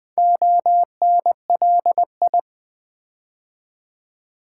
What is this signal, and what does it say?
Morse code, heard as a keyed tone, about 20 wpm, 690 Hz. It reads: ONLI
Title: Build A Word - 2 Letters At A Time - 6-Letter Words - 20wpm